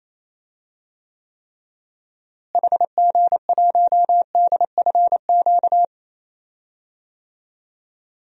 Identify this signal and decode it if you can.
Morse code, heard as a keyed tone, about 28 wpm, 700 Hz. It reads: HG1DFQ
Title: DXCC-100 Call Signs - 6 Characters - Repeated 3x - 28wpm